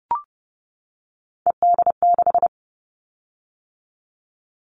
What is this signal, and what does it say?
Morse code, heard as a keyed tone, about 30 wpm, 700 Hz. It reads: ED6
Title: Callsign Prefixes for Top 50 DXCC - 30wpm